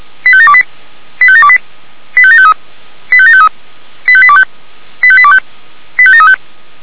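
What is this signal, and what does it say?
Five-tone selective call (ZVEI), heard as quick runs of five tones, tones of about 70 ms. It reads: 85728 85728 85753 85753 85825 85825 85735
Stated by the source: ZVEI five-tone selective call received on an SDR.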